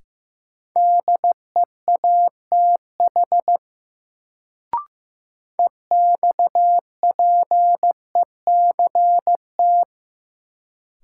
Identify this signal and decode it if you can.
Morse code, heard as a keyed tone, about 15 wpm, 700 Hz. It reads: DEATH EXPECT
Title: Sentences from Top 1100 Words - New Word Drill Alt - 15wpm